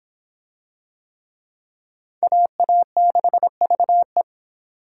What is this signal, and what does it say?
Morse code, heard as a keyed tone, about 25 wpm, 700 Hz. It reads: AA6VE